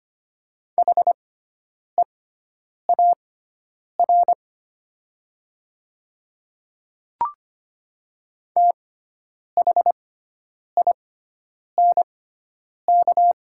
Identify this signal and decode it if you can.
Morse code, heard as a keyed tone, about 25 wpm, spaced as for 9 wpm, 700 Hz. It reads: HEAR THINK
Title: 6x Character Spacing - Top 300 Words - 25wpm